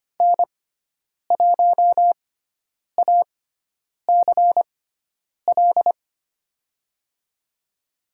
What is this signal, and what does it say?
Morse code, heard as a keyed tone, about 25 wpm, spaced as for 9 wpm, 700 Hz. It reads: N1ACL
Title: USA Call Signs - 5 Characters - 6x Character Spacing - 25wpm